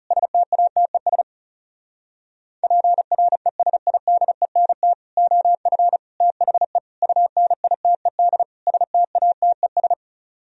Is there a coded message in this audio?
Yes: Morse code from 0.1 to 9.9 s, heard as a keyed tone, about 40 wpm, 700 Hz. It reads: STATES PRESIDENT OF THE UNITED STATES